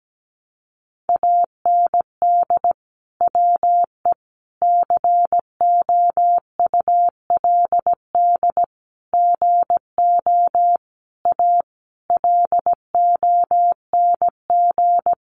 Morse code, heard as a keyed tone, about 17 wpm, 700 Hz. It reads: AND WE COULD GO A LONG
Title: Sentences from Top 1200 Words 17wpm